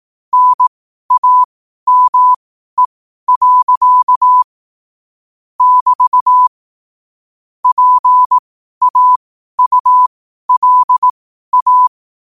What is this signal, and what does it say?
Morse code, heard as a keyed tone, about 18 wpm, spaced as for 12 wpm, 1000 Hz. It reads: NAME. <BT> PAULA
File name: qotdfiles / 12wpm